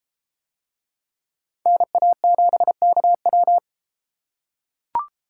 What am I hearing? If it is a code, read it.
Morse code, heard as a keyed tone, about 33 wpm, 700 Hz. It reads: NA7KW